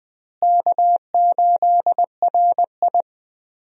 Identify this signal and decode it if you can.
Morse code, heard as a keyed tone, about 20 wpm, 700 Hz. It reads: K8RI
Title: USA Call Signs - 4 Characters - Repeated 3x - 20wpm